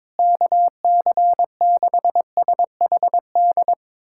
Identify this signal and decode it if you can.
Morse code, heard as a keyed tone, about 22 wpm, 700 Hz. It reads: KC6SHD
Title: USA Callsign DE Callsign 22wpm